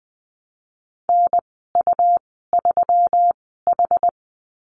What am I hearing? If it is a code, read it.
Morse code, heard as a keyed tone, about 20 wpm, spaced as for 15 wpm, 700 Hz. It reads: NU3H
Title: 122 Lesson 20g - Number 3 - Call Signs 20wpm